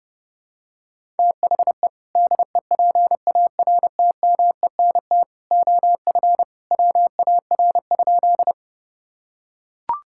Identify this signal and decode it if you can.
Morse code, heard as a keyed tone, about 32 wpm, 700 Hz. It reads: THE DEPARTMENT OF WAR?
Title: Sentences from Top 1000 Words - Review Part 5 - 30wpm